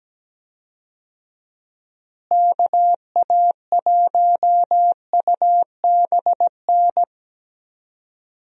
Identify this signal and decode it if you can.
Morse code, heard as a keyed tone, about 17 wpm, 700 Hz. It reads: KA1UBN
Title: USA Call Signs - 6 Characters - Repeated 2x - 17wpm